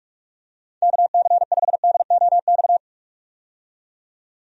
Morse code, heard as a keyed tone, about 46 wpm, 700 Hz. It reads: KC5DOX
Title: USA Call Signs - 6 Characters - 45wpm